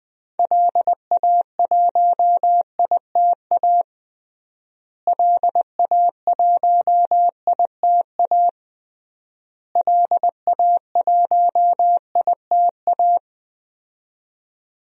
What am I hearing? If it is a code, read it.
Morse code, heard as a keyed tone, about 21 wpm, 700 Hz. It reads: LA1ITA LA1ITA LA1ITA
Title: DXCC-100 Call Signs - 6 Characters - Repeated 3x - 20wpm